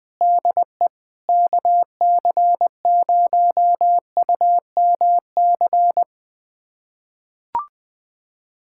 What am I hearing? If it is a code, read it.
Morse code, heard as a keyed tone, about 20 wpm, 700 Hz. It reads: DE KC0UMC
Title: USA Callsign DE Callsign - 6 Characters - 20wpm